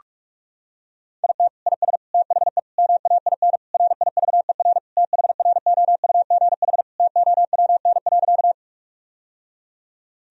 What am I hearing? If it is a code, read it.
Morse code, heard as a keyed tone, about 46 wpm, 700 Hz. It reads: IT IS THE MAIN RIVER THROUGH TOWN.
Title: Sentences from Top 1000 Words - Review Part 2 - 45wpm